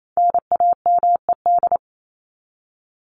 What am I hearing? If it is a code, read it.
Morse code, heard as a keyed tone, about 28 wpm, 700 Hz. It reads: NAMED
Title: ED Suffix - Top 100 Words with Base Words 28wpm